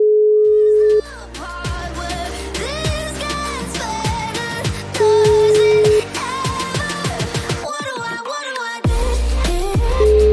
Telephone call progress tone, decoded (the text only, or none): ringback tone